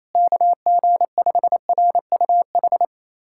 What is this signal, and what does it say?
Morse code, heard as a keyed tone, about 28 wpm, 700 Hz. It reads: KG5RUH
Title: USA Callsign DE Callsign - 6 Characters - 28wpm